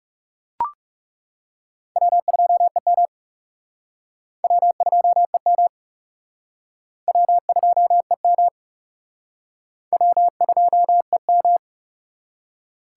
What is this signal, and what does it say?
Morse code, heard as a keyed tone, about 37 wpm, 700 Hz. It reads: W2EM W2EM W2EM W2EM
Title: Speed Racing - USA 1x2 Call Signs 30wpm